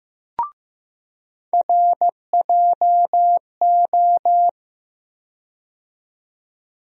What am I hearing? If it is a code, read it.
Morse code, heard as a keyed tone, about 15 wpm, 700 Hz. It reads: RJO
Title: USA Call Sign Postfixes - 3 Letters 15wpm